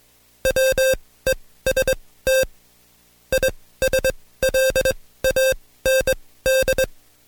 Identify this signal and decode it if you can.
Morse code, heard as a keyed tone, about 22 wpm, spaced as for 16 wpm, 530 Hz. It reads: WEST ISLAND
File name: Code Quick / AudioDownld2 / Seg134-Stepper7 15WPM